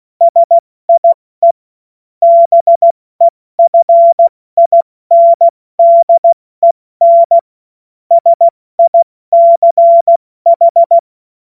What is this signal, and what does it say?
Morse code, heard as a keyed tone, about 16 wpm, spaced as for 14 wpm, 680 Hz. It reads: SIE BEFINDEN SICH
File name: Morsekurs / Witze / Witz 03. Mai 2022 70 F80